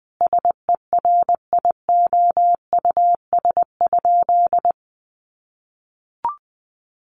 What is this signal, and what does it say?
Morse code, heard as a keyed tone, about 20 wpm, 700 Hz. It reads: SERIOUS?
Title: Sentences from Top 1100 Words 20wpm